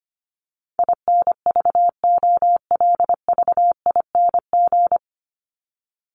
Morse code, heard as a keyed tone, about 25 wpm, 700 Hz. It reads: INVOLVING
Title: ING Suffix - Top 100 Words with Base Words 25wpm